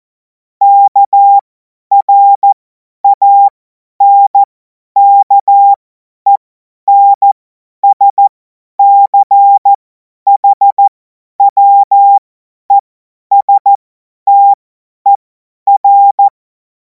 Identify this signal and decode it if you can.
Morse code, heard as a keyed tone, about 14 wpm, spaced as for 10 wpm, 800 Hz. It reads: KRANKENSCHWESTER
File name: Morsekurs / Witze / Witz 09. Juni 2026 50 F70